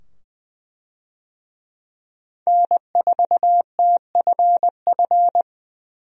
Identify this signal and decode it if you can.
Morse code, heard as a keyed tone, about 20 wpm, 700 Hz. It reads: N4TFF
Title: USA Call Signs - 5 Characters - 20wpm